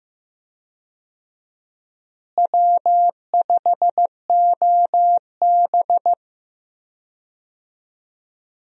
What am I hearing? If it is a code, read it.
Morse code, heard as a keyed tone, about 15 wpm, 700 Hz. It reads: W5OB
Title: USA Call Signs - 4 Characters - Repeated 2x - 15wpm